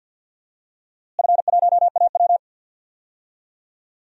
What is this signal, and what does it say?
Morse code, heard as a keyed tone, about 52 wpm, 700 Hz. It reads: F1AW
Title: DXCC-100 Call Signs - 4 Characters - Repeated 2x - 50wpm